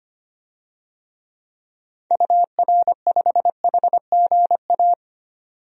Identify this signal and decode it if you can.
Morse code, heard as a keyed tone, about 25 wpm, 690 Hz. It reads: UR5HGA
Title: DXCC-100 Call Signs - 6 Characters - 25wpm